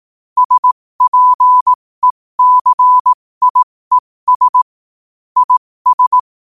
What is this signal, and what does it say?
Morse code, heard as a keyed tone, about 18 wpm, spaced as for 15 wpm, 1000 Hz. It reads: SPECIES IS
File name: qotdfiles / 15wpm